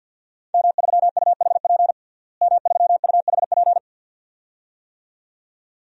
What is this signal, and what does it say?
Morse code, heard as a keyed tone, about 48 wpm, 700 Hz. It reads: M3UHP M3UHP
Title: DXCC-100 Call Signs - 5 Characters - Repeated 2x - 50wpm